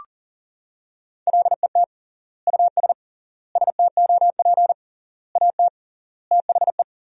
Morse code, heard as a keyed tone, about 37 wpm, 700 Hz. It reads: LET US STOP AT THE
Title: 3x Word-Spacing - Sentences from Top 400 Words - 40wpm